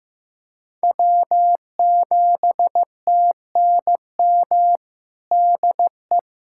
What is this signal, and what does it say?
Morse code, heard as a keyed tone, about 16 wpm, 700 Hz. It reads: W7TNM DE
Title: USA Callsign DE Callsign - 5 Characters - 15wpm